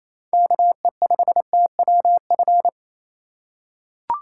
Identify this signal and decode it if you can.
Morse code, heard as a keyed tone, about 28 wpm, 700 Hz. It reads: KE5TWF
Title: USA Call Signs - 6 Characters - 4x Character Spacing - 28wpm